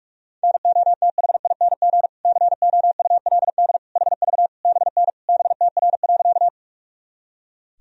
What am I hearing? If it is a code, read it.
Morse code, heard as a keyed tone, about 46 wpm, 700 Hz. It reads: NOTHING COULD HV BN BTR.
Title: Sentences from Top 500 Words and Abbreviations 45wpm